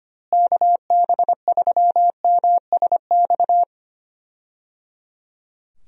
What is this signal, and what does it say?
Morse code, heard as a keyed tone, about 25 wpm, 700 Hz. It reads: KB3MSX
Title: USA Call Signs - 6 Characters - Repeated 3x - 25wpm